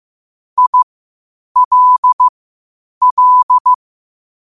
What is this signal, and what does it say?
Morse code, heard as a keyed tone, about 15 wpm, spaced as for 8 wpm, 1000 Hz. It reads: ILL
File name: qotdfiles / 8wpm